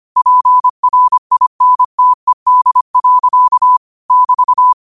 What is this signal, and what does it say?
Morse code, heard as a keyed tone, about 25 wpm, 1000 Hz. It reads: PRINTED. <BT>
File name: qotdfiles / 25wpm